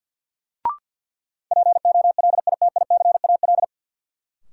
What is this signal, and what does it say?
Morse code, heard as a keyed tone, about 52 wpm, 700 Hz. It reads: POLITICAL
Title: Sentences from Top 800 Words - New Word Drill - 50wpm